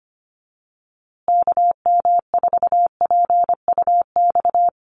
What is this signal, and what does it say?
Morse code, heard as a keyed tone, about 25 wpm, 700 Hz. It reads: KM4PUX
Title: USA Call Signs Repeated 3x 25wpm